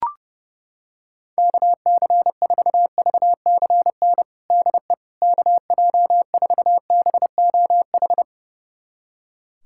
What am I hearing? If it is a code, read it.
Morse code, heard as a keyed tone, about 30 wpm, 700 Hz. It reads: KC4VCN DE KJ4BOH
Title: USA Callsign DE Callsign - 6 Characters - 30wpm